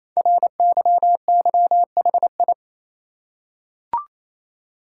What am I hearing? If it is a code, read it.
Morse code, heard as a keyed tone, about 27 wpm, 700 Hz. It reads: RYYHI
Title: Code Groups - 5 Letters - 28wpm